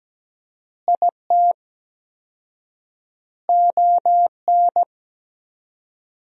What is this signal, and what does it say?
Morse code, heard as a keyed tone, about 17 wpm, 700 Hz. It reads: IT ON
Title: 4x Word-Spacing - Sentences from Top 400 Words - 17wpm